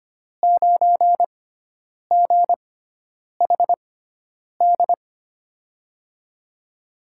Morse code, heard as a keyed tone, about 25 wpm, spaced as for 9 wpm, 700 Hz. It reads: 9GHD
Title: USA Call Signs - 5 Characters - 6x Character Spacing - 25wpm